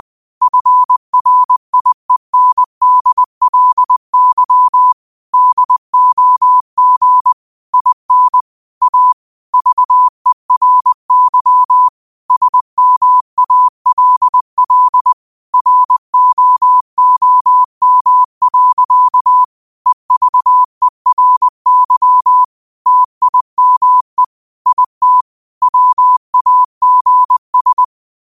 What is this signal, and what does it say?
Morse code, heard as a keyed tone, about 20 wpm, 1000 Hz. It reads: FRIENDLY DOG IN A VERY SMALL ROOM. EVERY TIME IT WAGS